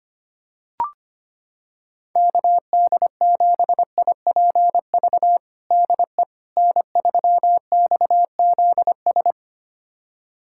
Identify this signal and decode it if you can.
Morse code, heard as a keyed tone, about 25 wpm, 700 Hz. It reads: KD7IPV DE N3XZS
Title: USA Callsign DE Callsign 25wpm